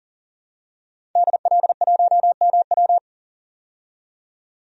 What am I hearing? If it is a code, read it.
Morse code, heard as a keyed tone, about 40 wpm, 700 Hz. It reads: DL1MW
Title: DXCC-100 Call Signs - 5 Characters - 40wpm